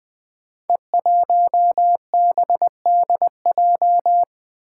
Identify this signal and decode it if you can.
Morse code, heard as a keyed tone, about 20 wpm, 700 Hz. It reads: E1BDJ
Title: Code Groups - 5 Letter:Numbers - 20wpm